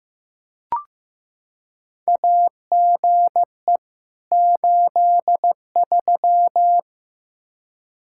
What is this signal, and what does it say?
Morse code, heard as a keyed tone, about 15 wpm, 700 Hz. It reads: AGE 83
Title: Practice QSO - Exchange 3 - More Info - 15wpm